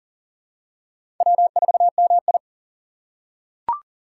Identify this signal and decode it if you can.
Morse code, heard as a keyed tone, about 40 wpm, 700 Hz. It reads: W4MI
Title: USA Call Signs - 3x Character Spacing - 40wpm